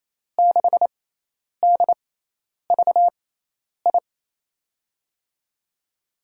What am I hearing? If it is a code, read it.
Morse code, heard as a keyed tone, about 28 wpm, spaced as for 10 wpm, 700 Hz. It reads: 6DVI
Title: USA Call Signs - 5 Characters - 6x Character Spacing - 28wpm